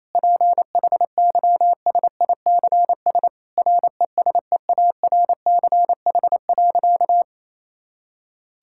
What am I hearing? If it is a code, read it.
Morse code, heard as a keyed tone, about 28 wpm, 700 Hz. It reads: PHYSICS RESEARCH.
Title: Sentences from Top 1000 Words - Review Part 5 - 28wpm